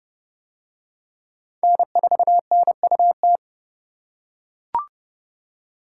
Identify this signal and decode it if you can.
Morse code, heard as a keyed tone, about 28 wpm, 700 Hz. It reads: N4NUT